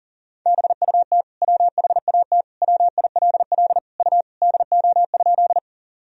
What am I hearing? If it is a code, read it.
Morse code, heard as a keyed tone, about 40 wpm, 700 Hz. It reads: BUT WHAT WILL U DO?
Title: Sentences from Top 100 Words and Abbreviations 40wpm